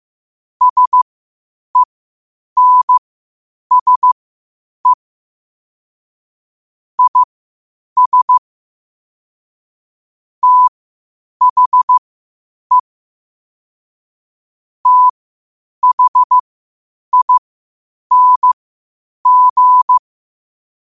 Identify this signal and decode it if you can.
Morse code, heard as a keyed tone, about 15 wpm, spaced as for 8 wpm, 1000 Hz. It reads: SENSE IS THE THING
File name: qotdfiles / 8wpm